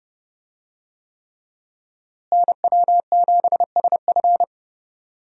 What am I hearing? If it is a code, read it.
Morse code, heard as a keyed tone, about 30 wpm, 700 Hz. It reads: NW7SF
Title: USA Call Signs - 5 Characters - Repeated 3x - 30wpm